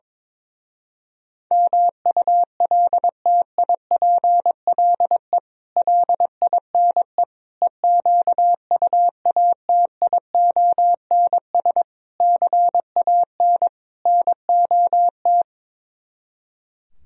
Morse code, heard as a keyed tone, about 22 wpm, 700 Hz. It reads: MULTIPLE LINE EQUATIONS CAN NOT
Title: Speed-Racing - Sets of 5 Words 22wpm